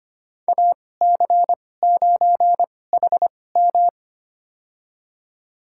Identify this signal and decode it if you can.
Morse code, heard as a keyed tone, about 25 wpm, spaced as for 18 wpm, 700 Hz. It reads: AC9HM
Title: USA Call Signs - 5 Characters - 2x Character Spacing - 25wpm